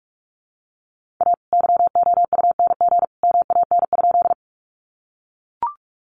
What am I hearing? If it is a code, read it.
Morse code, heard as a keyed tone, about 44 wpm, 700 Hz. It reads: A YOUNG MAN?
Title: Sentences from Top 400 Words 45wpm